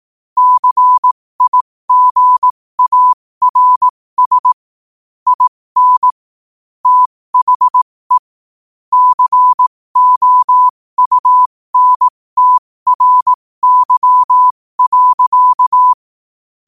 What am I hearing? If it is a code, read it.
Morse code, heard as a keyed tone, about 18 wpm, spaced as for 15 wpm, 1000 Hz. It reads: CIGARS IN THE COUNTRY.
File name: qotdfiles / 15wpm